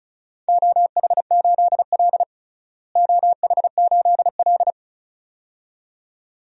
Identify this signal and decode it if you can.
Morse code, heard as a keyed tone, about 36 wpm, 700 Hz. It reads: OH8L OH8L
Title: DXCC-100 Call Signs - 4 Characters - Repeated 2x - 35wpm